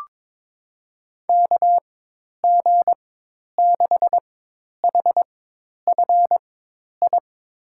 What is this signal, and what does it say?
Morse code, heard as a keyed tone, about 22 wpm, spaced as for 10 wpm, 700 Hz. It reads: KG6HFI
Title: USA Call Signs - 4x Character Spacing - 22wpm